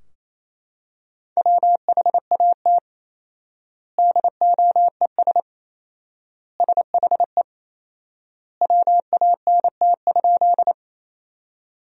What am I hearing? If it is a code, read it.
Morse code, heard as a keyed tone, about 28 wpm, 700 Hz. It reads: WHAT DOES SHE WANT?
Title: 4x Word-Spacing - Sentences from Top 200 Words - 28wpm